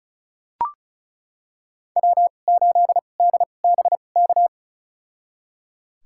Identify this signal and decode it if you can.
Morse code, heard as a keyed tone, about 34 wpm, spaced as for 25 wpm, 700 Hz. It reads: W8DBK
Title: USA Call Signs - 5 Characters - 2x Character Spacing - 35wpm